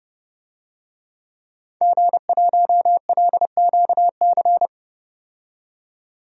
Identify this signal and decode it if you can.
Morse code, heard as a keyed tone, about 29 wpm, 700 Hz. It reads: G1LQC
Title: DXCC-100 Call Signs - 5 Characters - 30wpm